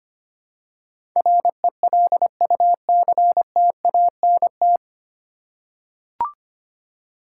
Morse code, heard as a keyed tone, about 25 wpm, 700 Hz. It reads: RELUCTANT
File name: practice / 25wpm / 9 Letter Words - First 3 Letters Given - 25wpm